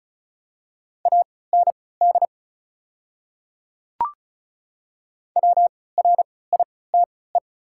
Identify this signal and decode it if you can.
Morse code, heard as a keyed tone, about 36 wpm, spaced as for 20 wpm, 700 Hz. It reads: AND WRITE
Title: 3x Character Spacing - Top 100 Words - 35wpm